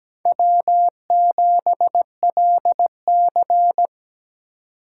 Morse code, heard as a keyed tone, about 20 wpm, 700 Hz. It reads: W7LC
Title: Speed Racing - USA 1x2 Call Signs 17wpm